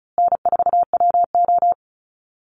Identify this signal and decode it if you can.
Morse code, heard as a keyed tone, about 34 wpm, 700 Hz. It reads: N4WO
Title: Easy Call Signs Repeated 3x - 35wpm